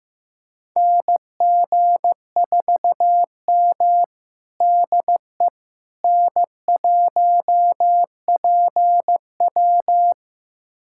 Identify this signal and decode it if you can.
Morse code, heard as a keyed tone, about 15 wpm, 700 Hz. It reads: NG4M DE N1PW
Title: USA Callsign DE Callsign - 4 Characters - 15wpm